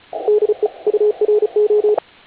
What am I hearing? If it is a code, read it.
Morse code, heard as a keyed tone, about 34 wpm, 420 Hz, with radio noise. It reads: DE URO